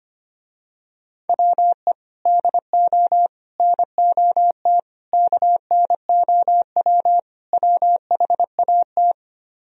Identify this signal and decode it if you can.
Morse code, heard as a keyed tone, about 25 wpm, 700 Hz. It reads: WE DO NOT KNOW WHAT
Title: Speed-Racing - Sets of 5 Words 25wpm